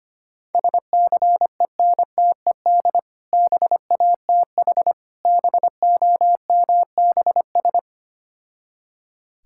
Morse code, heard as a keyed tone, about 25 wpm, 700 Hz. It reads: SCENTED BATH BOMBS
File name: practice / 25wpm / Difficult Words with Hints - 25wpm